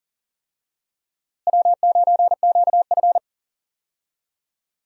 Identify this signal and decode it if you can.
Morse code, heard as a keyed tone, about 40 wpm, 700 Hz. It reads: W9QF
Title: USA Call Signs - 4 Characters - Repeated 3x - 40wpm